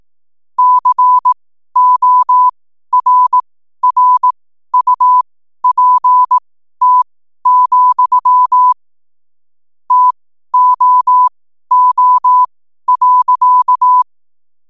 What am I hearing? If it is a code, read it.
Morse code, heard as a keyed tone, about 18 wpm, spaced as for 12 wpm, 1000 Hz. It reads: CORRUPT, TOO.
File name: n7net / 12wpm